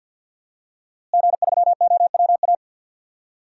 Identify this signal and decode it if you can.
Morse code, heard as a keyed tone, about 46 wpm, 700 Hz. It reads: G3OWA
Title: DXCC-100 Call Signs - 5 Characters - Repeated 3x - 50wpm